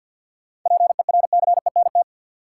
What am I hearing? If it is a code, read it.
Morse code, heard as a keyed tone, about 52 wpm, 700 Hz. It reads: PERCENT
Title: Sets of 2 Words - New Word Drill - 50wpm